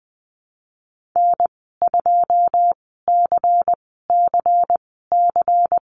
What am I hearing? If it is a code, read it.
Morse code, heard as a keyed tone, about 20 wpm, spaced as for 15 wpm, 700 Hz. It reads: N2CCC
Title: 154 Lesson 25g - Letter p - Call Signs 20wpm